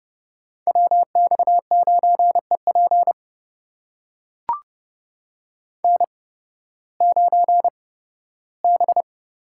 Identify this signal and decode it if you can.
Morse code, heard as a keyed tone, about 30 wpm, spaced as for 16 wpm, 700 Hz. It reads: WX9EP N9B
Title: USA Call Signs - 5 Characters - 8x Character Spacing - 30wpm